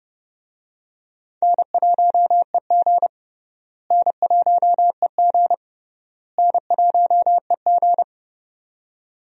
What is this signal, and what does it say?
Morse code, heard as a keyed tone, about 32 wpm, 700 Hz. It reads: N1EG N1EG N1EG
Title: USA Call Signs - 4 Characters - Repeated 3x - 30wpm